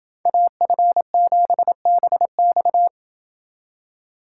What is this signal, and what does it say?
Morse code, heard as a keyed tone, about 29 wpm, 700 Hz. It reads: AF7BX